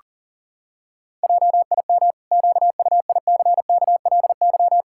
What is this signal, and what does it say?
Morse code, heard as a keyed tone, about 40 wpm, 700 Hz. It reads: JIM QUICKLY
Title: Pangram Challenge v2 - 40wpm